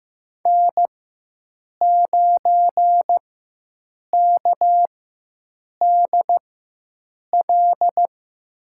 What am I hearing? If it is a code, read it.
Morse code, heard as a keyed tone, about 15 wpm, spaced as for 7 wpm, 700 Hz. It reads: N9KDL
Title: USA Call Signs - 5 Characters - 4x Character Spacing - 15wpm